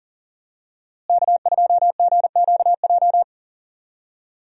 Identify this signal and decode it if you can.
Morse code, heard as a keyed tone, about 40 wpm, 700 Hz. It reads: K2GQJ
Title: USA Call Signs - 5 Characters - Repeated 2x - 40wpm